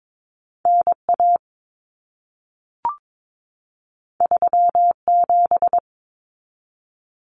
Morse code, heard as a keyed tone, about 22 wpm, 700 Hz. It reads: NA 37
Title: Common Cut Numbers - 2-Digit 22wpm